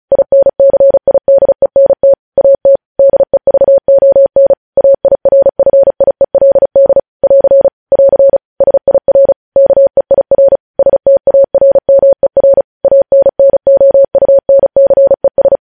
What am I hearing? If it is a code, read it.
Morse code, heard as a keyed tone, about 36 wpm, 550 Hz. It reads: INCIDENT AT DEVON AIRFIELD <AR> <AR> SIR KEIR STARMER ANNOUNCES
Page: news headlines in morse code at 35 wpm